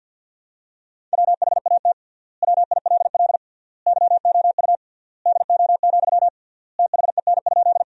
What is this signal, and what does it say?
Morse code, heard as a keyed tone, about 48 wpm, 700 Hz. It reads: WHAT WILL YOU DO, THEN?
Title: 3x Word-Spacing - Sentences from Top 100 Words - 50wpm